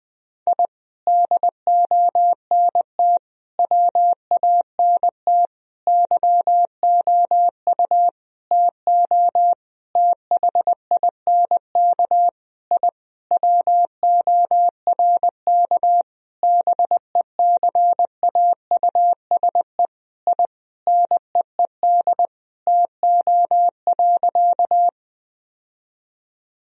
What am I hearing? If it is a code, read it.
Morse code, heard as a keyed tone, about 20 wpm, 700 Hz. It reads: I DONT WANT YOU TO THINK I WORK BECAUSE I NEED TO.
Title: Sentences from Top 1000 Words - Review Part 3 - 20wpm